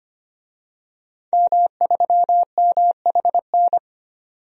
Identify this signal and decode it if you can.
Morse code, heard as a keyed tone, about 25 wpm, 700 Hz. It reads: M3MHN
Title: DXCC-100 Call Signs - 5 Characters - 25wpm